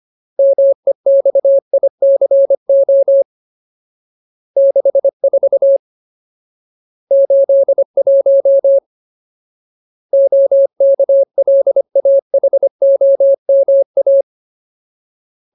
Morse code, heard as a keyed tone, about 25 wpm, 550 Hz. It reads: MEXICO 64 81 OKLAHOMA
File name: Ultimate Word List / 25WPM Ultimate